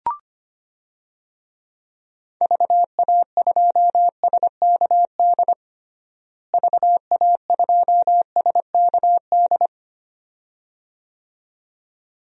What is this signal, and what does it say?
Morse code, heard as a keyed tone, about 25 wpm, 700 Hz. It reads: VA2SKD VA2SKD
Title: DXCC-100 Call Signs - 6 Characters - Repeated 2x - 25wpm